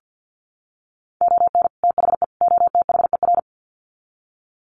Morse code, heard as a keyed tone, about 48 wpm, 700 Hz. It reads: ON THE OTHER
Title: Sets of 3 Words 50wpm